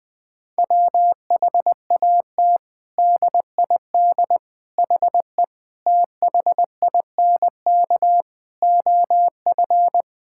Morse code, heard as a keyed tone, about 20 wpm, 700 Hz. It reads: WHAT DID HE THINK OF
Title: Sentences from Top 1000 Words - Review Part 6 - 20wpm